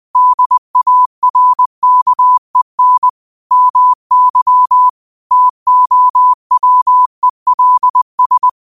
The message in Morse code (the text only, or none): DARKEN MY TOWELS